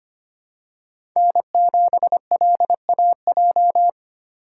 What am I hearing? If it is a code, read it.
Morse code, heard as a keyed tone, about 25 wpm, 700 Hz. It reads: N7LAJ
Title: USA Call Signs - 5 Characters - 2x Character Spacing - 25wpm